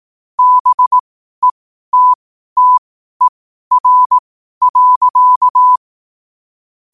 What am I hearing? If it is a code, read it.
Morse code, heard as a keyed tone, about 18 wpm, spaced as for 12 wpm, 1000 Hz. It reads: BETTER.
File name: qotdfiles / 12wpm